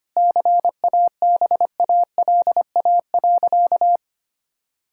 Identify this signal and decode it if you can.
Morse code, heard as a keyed tone, about 25 wpm, 700 Hz. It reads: CABALA.
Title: Pangram Challenge v2 - 25wpm